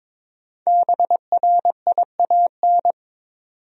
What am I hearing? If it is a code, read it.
Morse code, heard as a keyed tone, about 21 wpm, 700 Hz. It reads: BRIAN
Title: USA Common Names - Max 5 Letters - 22wpm